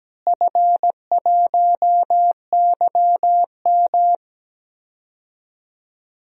Morse code, heard as a keyed tone, about 17 wpm, 700 Hz. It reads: F1YM
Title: DXCC-100 Call Signs - 4 Characters - Repeated 3x - 17wpm